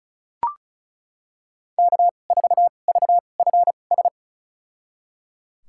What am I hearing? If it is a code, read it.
Morse code, heard as a keyed tone, about 36 wpm, spaced as for 25 wpm, 700 Hz. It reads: K4VFS
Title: USA Call Signs - 5 Characters - 2x Character Spacing - 35wpm